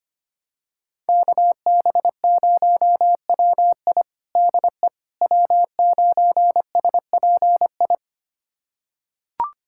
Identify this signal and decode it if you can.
Morse code, heard as a keyed tone, about 25 wpm, 700 Hz. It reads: KB0WI DE W9SPI
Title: USA Callsign DE Callsign - 5 Characters - 25wpm